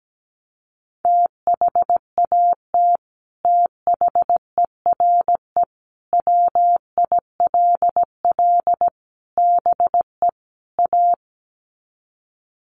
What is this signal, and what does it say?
Morse code, heard as a keyed tone, about 17 wpm, 700 Hz. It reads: THAT THERE WILL BE A
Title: Sets of 5 Words Encore2 17wpm